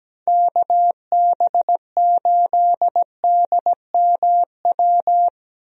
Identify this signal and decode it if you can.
Morse code, heard as a keyed tone, about 17 wpm, 700 Hz. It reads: KB8DMW
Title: USA Call Signs - 6 Characters - Repeated 2x - 17wpm